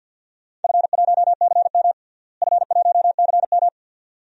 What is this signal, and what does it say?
Morse code, heard as a keyed tone, about 50 wpm, 700 Hz. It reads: F1CM F1CM
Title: DXCC-100 Call Signs - 4 Characters - Repeated 2x - 50wpm